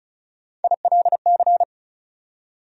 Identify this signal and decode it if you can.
Morse code, heard as a keyed tone, about 36 wpm, 700 Hz. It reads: ILC
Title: USA Call Sign Postfixes - 3 Letters 35wpm